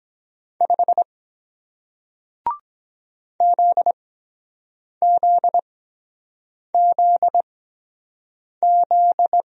Morse code, heard as a keyed tone, about 22 wpm, spaced as for 7 wpm, 700 Hz. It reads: 5 ZZZZ